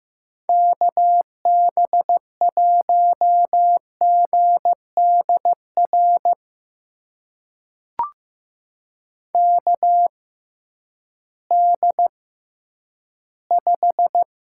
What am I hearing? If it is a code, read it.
Morse code, heard as a keyed tone, about 15 wpm, spaced as for 10 wpm, 700 Hz. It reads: KB1GDR KD5
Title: USA Call Signs - 6 Characters - 6x Character Spacing - 15wpm